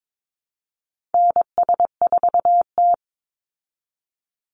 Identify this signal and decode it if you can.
Morse code, heard as a keyed tone, about 22 wpm, 690 Hz. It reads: NS4T
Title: Call Signs Repeated 3x - 22wpm